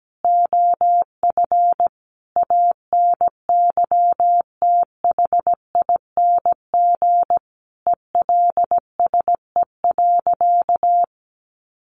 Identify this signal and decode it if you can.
Morse code, heard as a keyed tone, about 17 wpm, 700 Hz. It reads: OF ANYTHING ELSE.
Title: Sentences from Top 1500 Words 17wpm